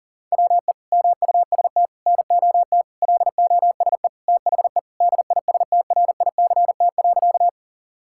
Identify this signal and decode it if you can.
Morse code, heard as a keyed tone, about 40 wpm, 700 Hz. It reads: WE MUST NOT LOSE THE DISTRICT.
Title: Sentences from Top 1600 Words 40wpm